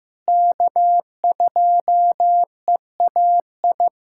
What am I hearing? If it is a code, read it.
Morse code, heard as a keyed tone, about 16 wpm, 700 Hz. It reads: K2EAI
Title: USA Callsign DE Callsign - 5 Characters - 15wpm